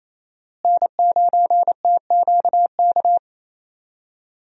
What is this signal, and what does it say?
Morse code, heard as a keyed tone, about 28 wpm, 700 Hz. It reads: N9TQK
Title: USA Call Signs - 5 Characters - Repeated 3x - 28wpm